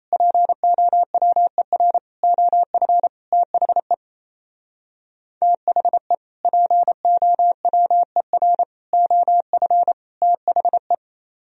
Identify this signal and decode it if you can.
Morse code, heard as a keyed tone, about 38 wpm, 700 Hz. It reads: POWER OF THE THE POWER OF THE